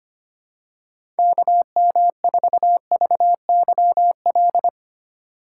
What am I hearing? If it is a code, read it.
Morse code, heard as a keyed tone, about 25 wpm, 700 Hz. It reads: KM4VYL
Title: USA Call Signs - 6 Characters - 2x Character Spacing - 25wpm